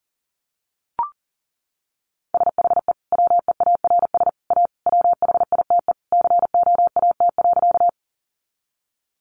Morse code, heard as a keyed tone, about 37 wpm, 700 Hz. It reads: SHE WEARS A WHITE COAT.
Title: Additional Easy Phrases 40wpm